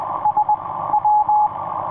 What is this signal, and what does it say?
Morse code, heard as a keyed tone, about 20 wpm, 840 Hz, with radio noise. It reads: S W
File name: kp4ms it9esw etc 30m cw 27jan2010 2226